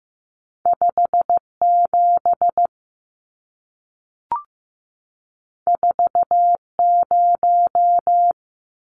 Morse code, heard as a keyed tone, about 15 wpm, 700 Hz. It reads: 57 40
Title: Numbers - 2-Digit 15wpm